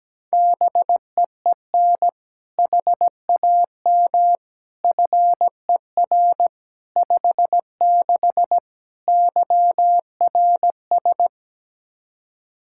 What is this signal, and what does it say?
Morse code, heard as a keyed tone, about 17 wpm, 700 Hz. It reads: BEEN HAM FER 56 YRS
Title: Practice QSO - Exchange 3 - More Info - 17wpm